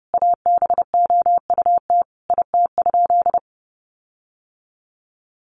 Morse code, heard as a keyed tone, about 28 wpm, 700 Hz. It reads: ABOUT IT?
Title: Sentences from Top 1500 Words 30wpm